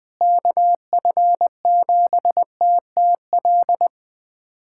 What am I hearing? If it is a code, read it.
Morse code, heard as a keyed tone, about 20 wpm, 700 Hz. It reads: KF7TTL
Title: USA Callsign DE Callsign - 6 Characters - 20wpm